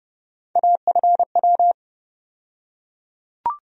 Morse code, heard as a keyed tone, about 30 wpm, 700 Hz. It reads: AFW
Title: USA Call Sign Postfixes - 3 Letters 30wpm